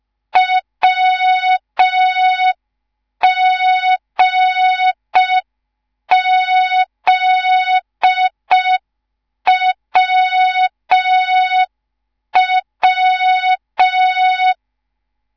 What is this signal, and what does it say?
Morse code, heard as a keyed tone, about 5 wpm, 3700 Hz. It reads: WGZWW